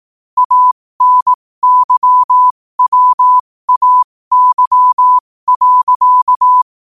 Morse code, heard as a keyed tone, about 18 wpm, spaced as for 15 wpm, 1000 Hz. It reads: ANYWAY.
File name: qotdfiles / 15wpm